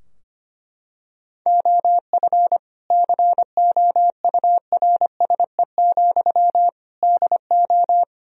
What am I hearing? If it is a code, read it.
Morse code, heard as a keyed tone, about 25 wpm, 700 Hz. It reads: OF COURSE, DO
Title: Sentences from Top 1000 Words - Review Part 3 - 25wpm